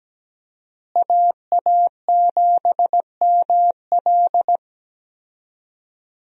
Morse code, heard as a keyed tone, about 17 wpm, 700 Hz. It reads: AA7ML
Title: USA Call Signs - 5 Characters - Repeated 3x - 17wpm